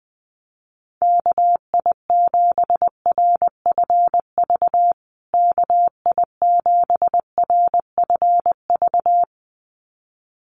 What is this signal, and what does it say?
Morse code, heard as a keyed tone, about 20 wpm, 700 Hz. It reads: KI7RFV KI7RFV
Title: USA Call Signs Repeated 2x 20wpm